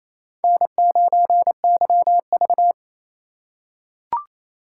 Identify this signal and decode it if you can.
Morse code, heard as a keyed tone, about 28 wpm, 700 Hz. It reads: N9YV
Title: USA Call Signs - 4 Characters - 2x Character Spacing - 28wpm